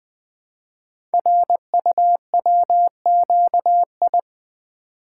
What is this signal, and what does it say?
Morse code, heard as a keyed tone, about 20 wpm, 700 Hz. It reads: RUWQI
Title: Code Groups - 5 Letters - 20wpm